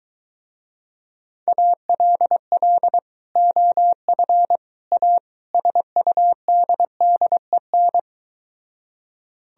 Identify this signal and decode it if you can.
Morse code, heard as a keyed tone, about 22 wpm, 700 Hz. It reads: ALL OF A SUDDEN